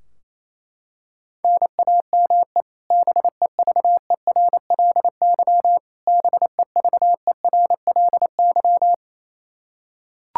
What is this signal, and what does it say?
Morse code, heard as a keyed tone, about 28 wpm, 700 Hz. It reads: NAME BEVERLY BEVERLY
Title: Practice QSO - Exchange 1 - Beginning - 28wpm